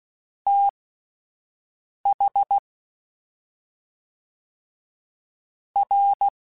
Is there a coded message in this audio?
Yes: Morse code from 0.5 to 6.3 s, heard as a keyed tone, about 16 wpm, spaced as for 4 wpm, 800 Hz. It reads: THR